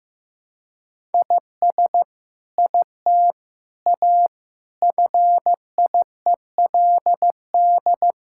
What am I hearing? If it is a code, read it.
Morse code, heard as a keyed tone, about 16 wpm, 700 Hz. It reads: IS IT A FIELD
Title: Sentences from Top 1000 Words - Review Part 5 - 15wpm